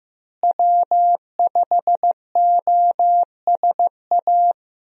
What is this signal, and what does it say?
Morse code, heard as a keyed tone, about 15 wpm, 700 Hz. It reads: W5OSA
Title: USA Call Signs - 5 Characters - Repeated 2x - 15wpm